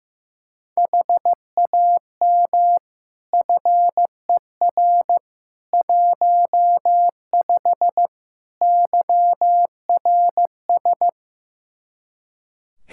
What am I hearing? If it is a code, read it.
Morse code, heard as a keyed tone, about 15 wpm, 700 Hz. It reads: HAM FER 15 YRS
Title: Practice QSO - Exchange 3 - More Info - 15wpm